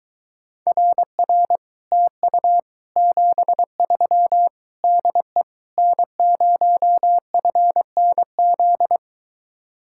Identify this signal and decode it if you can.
Morse code, heard as a keyed tone, about 23 wpm, 700 Hz. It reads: RR TU 73 DE N0FNZ